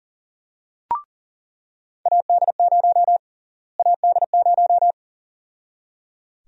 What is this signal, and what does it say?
Morse code, heard as a keyed tone, about 38 wpm, 700 Hz. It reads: AD0 AD0
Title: USA Call Sign Prefixes - Repeated 2x - 40wpm